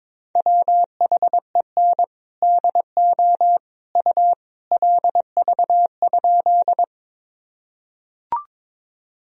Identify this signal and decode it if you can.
Morse code, heard as a keyed tone, about 22 wpm, 700 Hz. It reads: WHEN DO U LV?
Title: Sentences from Top 400 Words and Abbreviations 22wpm